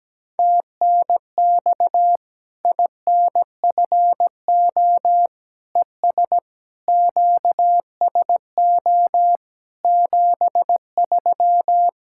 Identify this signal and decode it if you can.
Morse code, heard as a keyed tone, about 17 wpm, 700 Hz. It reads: TNX INFO ES QSO 73
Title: Practice QSO - Exchange 4 - Ending - 17wpm